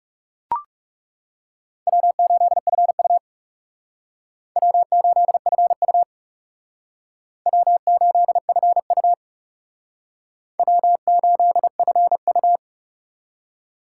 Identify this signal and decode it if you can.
Morse code, heard as a keyed tone, about 37 wpm, 700 Hz. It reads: W8FU W8FU W8FU W8FU
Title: Speed Racing - USA 1x2 Call Signs 30wpm